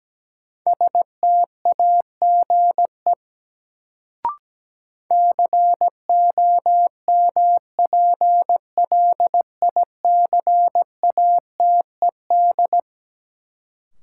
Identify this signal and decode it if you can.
Morse code, heard as a keyed tone, about 17 wpm, 700 Hz. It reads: STAGE COMPLICATED
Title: Sentences from Top 1200 Words - New Word Drill - 17wpm